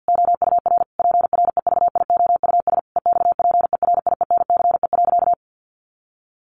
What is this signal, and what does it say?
Morse code, heard as a keyed tone, about 50 wpm, 700 Hz. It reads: OUR PREVIOUS EXPERIENCE.
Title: Sentences from Top 1500 Words 50wpm